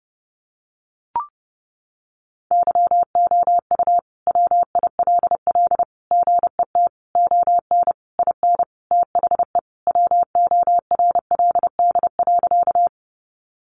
Morse code, heard as a keyed tone, about 30 wpm, 700 Hz. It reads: YOU WILL GET ON IN THE WORLD.
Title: Sentences from Top 200 Words 30wpm